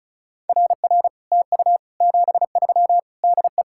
Morse code, heard as a keyed tone, about 45 wpm, 700 Hz. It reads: RR TU 73 DE